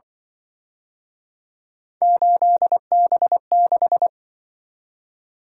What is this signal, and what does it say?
Morse code, heard as a keyed tone, about 24 wpm, 700 Hz. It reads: 8B6
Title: Speed-Racing - Three Letter-Number Combinations - 20wpm